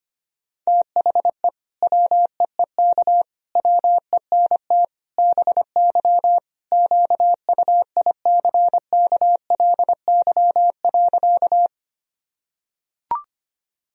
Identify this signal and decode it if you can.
Morse code, heard as a keyed tone, about 25 wpm, 700 Hz. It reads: THE WEEK WENT BY QUICKLY.
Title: Sentences from Top 1000 Words - Review Part 2 - 25wpm